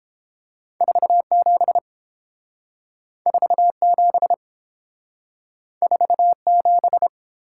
Morse code, heard as a keyed tone, about 29 wpm, 700 Hz. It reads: 47 47 47